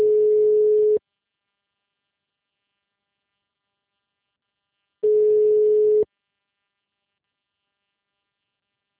Ringback tone (Europe, 425 Hz).